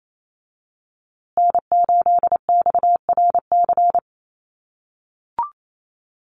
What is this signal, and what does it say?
Morse code, heard as a keyed tone, about 28 wpm, 700 Hz. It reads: N8XRC
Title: USA Call Signs Repeated 3x 28wpm